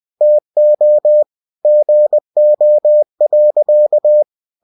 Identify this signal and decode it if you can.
Morse code, heard as a keyed tone, about 20 wpm, 600 Hz. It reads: TO GO.
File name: cw-books / winnie-the-pooh / winnie-20wpm-page-0049